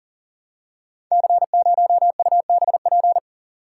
Morse code, heard as a keyed tone, about 38 wpm, 690 Hz. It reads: C0UBP
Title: Code Groups - 5 Letter:Numbers - 40wpm